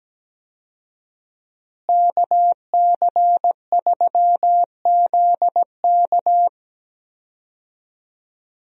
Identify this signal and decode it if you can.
Morse code, heard as a keyed tone, about 17 wpm, 700 Hz. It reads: KC3ZK
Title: USA Call Signs - 5 Characters - Repeated 2x - 17wpm